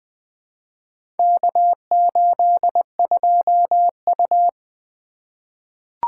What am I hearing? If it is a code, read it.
Morse code, heard as a keyed tone, about 20 wpm, 700 Hz. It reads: K82U
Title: Four Letter-Number Combinations - 20wpm